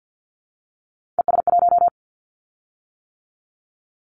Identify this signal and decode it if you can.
Morse code, heard as a keyed tone, about 50 wpm, 700 Hz. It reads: ES1
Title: Callsign Prefixes for Top 50 DXCC - 50wpm